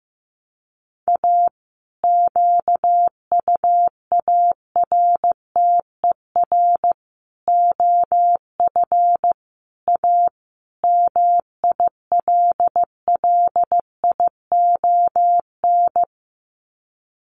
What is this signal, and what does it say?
Morse code, heard as a keyed tone, about 15 wpm, 700 Hz. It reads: A QUARTER OF A MILLION
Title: Sets of 5 Words Encore2 15wpm